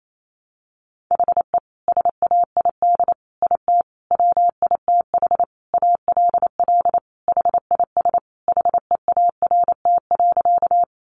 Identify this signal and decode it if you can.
Morse code, heard as a keyed tone, about 28 wpm, 700 Hz. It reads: HE SAID IT WITH ALL HIS HEART.
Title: Sentences from Top 1200 Words 28wpm